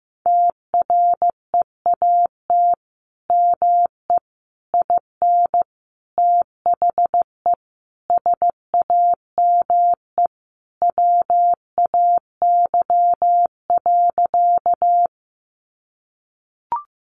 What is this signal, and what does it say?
Morse code, heard as a keyed tone, about 15 wpm, 700 Hz. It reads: TREAT ME IN THE SAME WAY.
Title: Sentences from Top 1300 Words 15wpm